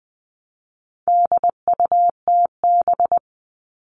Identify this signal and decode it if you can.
Morse code, heard as a keyed tone, about 20 wpm, 700 Hz. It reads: DUTB
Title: All Cut Numbers - 4-Digit 20wpm